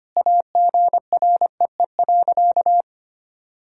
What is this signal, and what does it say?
Morse code, heard as a keyed tone, about 25 wpm, 700 Hz. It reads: AGREE.
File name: practice / 25wpm / Sentences with 3-Word Phrases - 25wpm